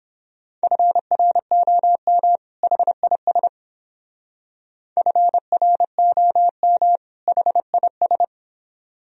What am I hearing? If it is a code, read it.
Morse code, heard as a keyed tone, about 28 wpm, 700 Hz. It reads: FROM HIS FROM HIS